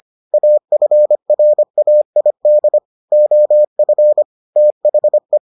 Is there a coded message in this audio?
Yes: Morse code from 0.3 to 5.4 s, heard as a keyed tone, about 25 wpm, 600 Hz. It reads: AFRAID OF THE